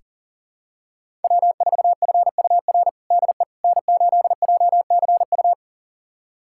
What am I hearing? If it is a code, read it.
Morse code, heard as a keyed tone, about 40 wpm, 700 Hz. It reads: W4FUR DE N8JCU
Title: USA Callsign DE Callsign - 5 Characters - 40wpm